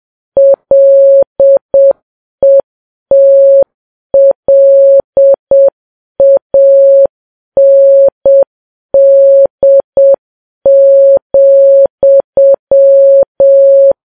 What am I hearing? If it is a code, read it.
Morse code, heard as a keyed tone, about 7 wpm, 550 Hz. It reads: LETLAND,